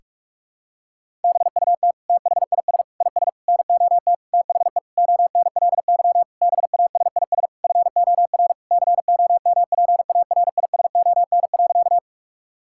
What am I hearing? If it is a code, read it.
Morse code, heard as a keyed tone, about 46 wpm, 700 Hz. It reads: BUT THIS IS NOT THE ONLY BASIS FOR COMPARISON.
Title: Sentences from Top 1600 Words 45wpm